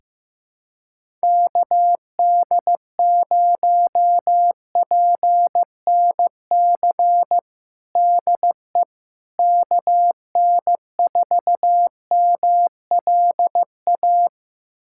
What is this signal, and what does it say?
Morse code, heard as a keyed tone, about 15 wpm, 700 Hz. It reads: KD0PNC DE KN4MLA
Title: USA Callsign DE Callsign - 6 Characters - 15wpm